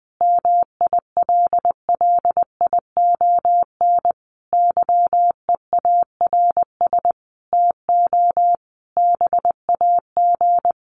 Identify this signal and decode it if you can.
Morse code, heard as a keyed tone, about 20 wpm, 700 Hz. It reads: MILLION YEARS TO BAG
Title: A Sound Of Thunder 20wpm